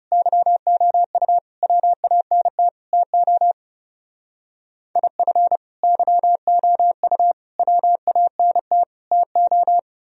Morse code, heard as a keyed tone, about 35 wpm, 700 Hz. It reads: YOU WANT TO IF YOU WANT TO
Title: Speed-Racing - Sets of 4 Words 30wpm